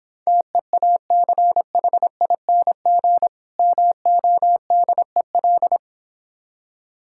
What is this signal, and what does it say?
Morse code, heard as a keyed tone, about 26 wpm, 700 Hz. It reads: TEACHING MODEL